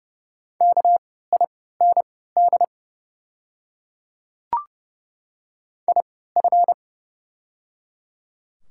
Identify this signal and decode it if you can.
Morse code, heard as a keyed tone, about 30 wpm, spaced as for 17 wpm, 700 Hz. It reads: KIND IF
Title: 3x Character Spacing - Top 300 Words - 30wpm